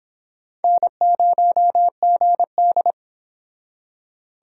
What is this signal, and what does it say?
Morse code, heard as a keyed tone, about 30 wpm, 700 Hz. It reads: N0GD